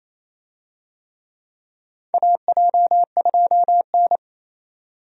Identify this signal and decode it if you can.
Morse code, heard as a keyed tone, about 28 wpm, 700 Hz. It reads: AJ2N
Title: USA Call Signs - 4 Characters - 28wpm